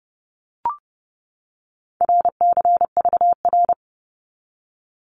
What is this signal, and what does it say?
Morse code, heard as a keyed tone, about 30 wpm, 700 Hz. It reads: RCVR
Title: Abbreviations 30wpm